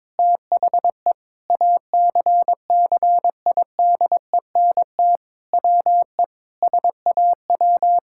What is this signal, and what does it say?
Morse code, heard as a keyed tone, about 22 wpm, 700 Hz. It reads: THE ACCIDENT WE SAW
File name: practice / 22wpm / Sentences with 4-Word Phrases - 22wpm